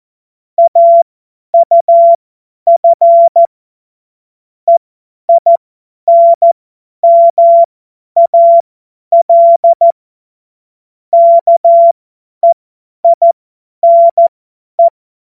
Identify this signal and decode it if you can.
Morse code, heard as a keyed tone, about 14 wpm, spaced as for 10 wpm, 680 Hz. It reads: AUF EINMAL KEINE
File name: Morsekurs / Witze / Witz 03. Januar 2023 50 F70